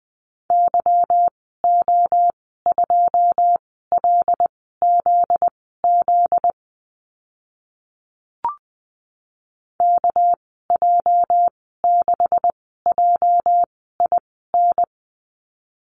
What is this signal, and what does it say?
Morse code, heard as a keyed tone, about 20 wpm, spaced as for 14 wpm, 700 Hz. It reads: YO2LZZ KJ6JIN
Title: 245 Lesson 39g - Letter z - Call Signs 20wpm